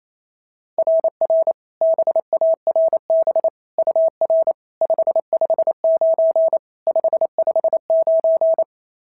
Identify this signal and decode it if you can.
Morse code, heard as a keyed tone, about 25 wpm, 650 Hz. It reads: RR BARB UR 559 559
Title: SOTA QSO Practice 28wpm